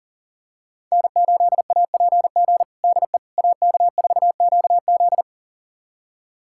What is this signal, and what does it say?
Morse code, heard as a keyed tone, about 40 wpm, 700 Hz. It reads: N8APG DE AK4QZ
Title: USA Callsign DE Callsign - 5 Characters - 40wpm